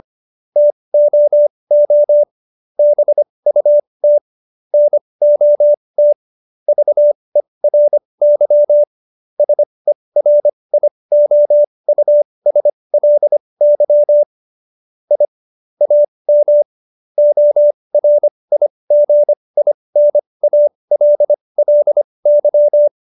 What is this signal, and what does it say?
Morse code, heard as a keyed tone, about 25 wpm, spaced as for 20 wpm, 590 Hz. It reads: TOO BUT NOT VERY SERIOUSLY I AM ORIGINALLY